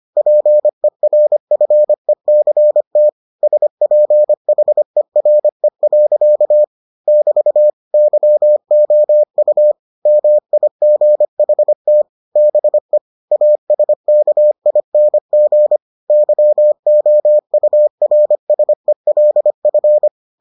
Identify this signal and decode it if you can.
Morse code, heard as a keyed tone, about 25 wpm, 600 Hz. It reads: PERFECT SPHERE. <BT> YOU MIGHT BE ASKING YOURSELF